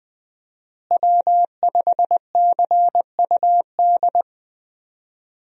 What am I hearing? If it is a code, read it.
Morse code, heard as a keyed tone, about 21 wpm, 700 Hz. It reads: W5CUD
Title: USA Call Signs - 5 Characters - Repeated 2x - 20wpm